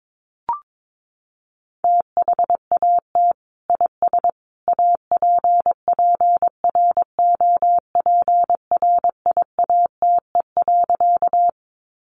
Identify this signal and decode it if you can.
Morse code, heard as a keyed tone, about 22 wpm, 700 Hz. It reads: THAT IS APPROPRIATE.
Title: Sentences from Top 900 Words 22wpm